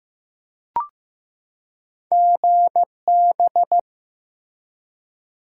Morse code, heard as a keyed tone, about 15 wpm, 700 Hz. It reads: GB
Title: Top 90 QSO Elements 15wpm